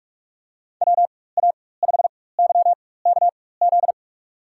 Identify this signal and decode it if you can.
Morse code, heard as a keyed tone, about 47 wpm, spaced as for 21 wpm, 700 Hz. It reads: WA5YKZ
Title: USA Call Signs - 4x Character Spacing - 45wpm